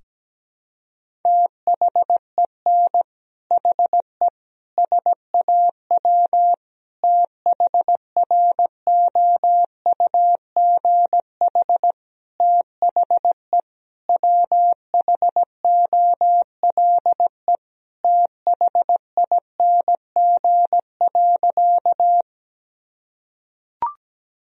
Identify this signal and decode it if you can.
Morse code, heard as a keyed tone, about 17 wpm, 700 Hz. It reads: THEN HE SAW THROUGH THE WHOLE THING.
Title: Sentences from Top 1000 Words - Review Part 1 - 17wpm